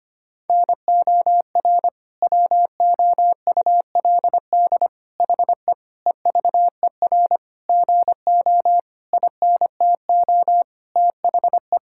Morse code, heard as a keyed tone, about 25 wpm, 700 Hz. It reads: NOR WOULD HE EVER GO INTO THE
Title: Sentences from Top 1600 Words 25wpm